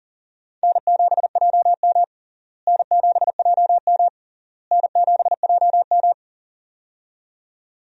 Morse code, heard as a keyed tone, about 40 wpm, 700 Hz. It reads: N7JM N7JM N7JM
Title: USA Call Signs - 4 Characters - Repeated 3x - 40wpm